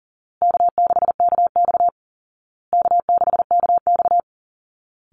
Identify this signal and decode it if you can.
Morse code, heard as a keyed tone, about 40 wpm, 700 Hz. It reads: K6KX K6KX
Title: Call Signs Repeated 2x - 40wpm